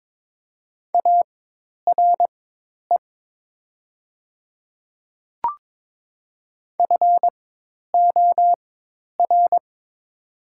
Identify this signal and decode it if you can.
Morse code, heard as a keyed tone, about 22 wpm, spaced as for 10 wpm, 700 Hz. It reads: ARE FOR
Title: 4x Character Spacing - Top 100 Words - 22wpm